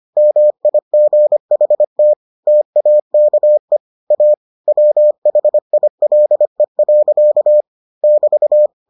Morse code, heard as a keyed tone, about 25 wpm, 600 Hz. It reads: MIGHT TAKE A WHILE. <BT>